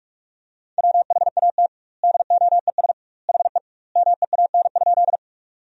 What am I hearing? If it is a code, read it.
Morse code, heard as a keyed tone, about 46 wpm, 700 Hz. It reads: WHAT DOES HE MEAN?
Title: 2x Word-Spacing - Sentences from Top 200 Words - 45wpm